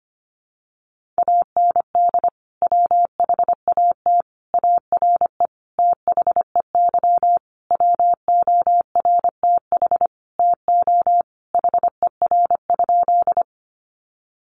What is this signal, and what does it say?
Morse code, heard as a keyed tone, about 25 wpm, 700 Hz. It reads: AND WHAT ARE THEY WORTH TO HER?
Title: Sentences from Top 800 Words 25wpm